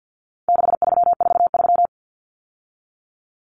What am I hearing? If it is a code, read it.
Morse code, heard as a keyed tone, about 50 wpm, 700 Hz. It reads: 6343
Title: Numbers - 4-Digit 50wpm